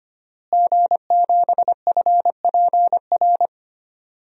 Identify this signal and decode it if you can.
Morse code, heard as a keyed tone, about 25 wpm, 700 Hz. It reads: G7FPR
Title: DXCC-100 Call Signs - 5 Characters - Repeated 2x - 25wpm